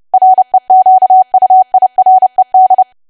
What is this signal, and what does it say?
Morse code, heard as a keyed tone, about 30 wpm, 750 Hz. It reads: REQUIRED
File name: cwa / W1AW-Level3 / 30WPM